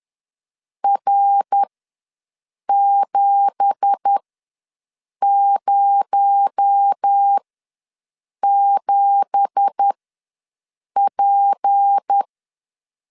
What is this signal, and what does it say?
Morse code, heard as a keyed tone, about 10 wpm, spaced as for 6 wpm, 800 Hz. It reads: R707P
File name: LEZIONI / 9 2867 ws 10wpm 4215